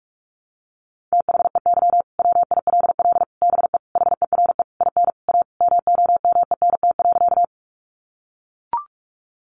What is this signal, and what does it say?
Morse code, heard as a keyed tone, about 44 wpm, 700 Hz. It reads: THEY WILL BE HERE IN A MOMENT.
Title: Sentences from Top 1000 Words 45wpm